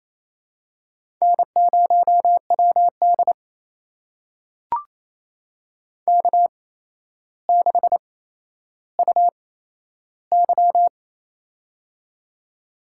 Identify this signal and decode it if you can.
Morse code, heard as a keyed tone, about 28 wpm, spaced as for 12 wpm, 700 Hz. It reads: N0WD K6UY
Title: USA Call Signs - 4 Characters - 8x Character Spacing - 28wpm